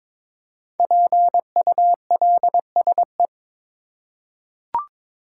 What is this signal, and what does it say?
Morse code, heard as a keyed tone, about 22 wpm, 690 Hz. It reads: PULSE
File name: 5 Letter Words - First 3 Letters Given - 22wpm